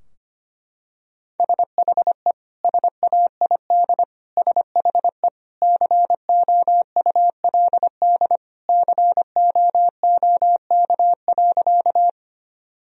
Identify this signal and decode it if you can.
Morse code, heard as a keyed tone, about 25 wpm, 700 Hz. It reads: SHE SAID SHE COULD COOK.
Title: Sentences from Top 1000 Words - Review Part 4 - 25wpm